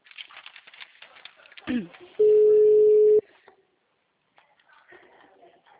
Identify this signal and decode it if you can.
Ringback tone (Europe, 425 Hz), with speech.
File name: out- -2008- - - .23431